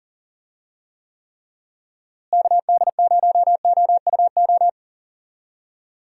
Morse code, heard as a keyed tone, about 40 wpm, 700 Hz. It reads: KD0OUO
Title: USA Call Signs - 6 Characters - Repeated 3x - 40wpm